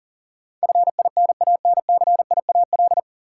Morse code, heard as a keyed tone, about 39 wpm, 700 Hz. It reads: FINANCIAL
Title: Sentences from Top 1000 Words - Review Part 5 - New Word Drill - 40wpm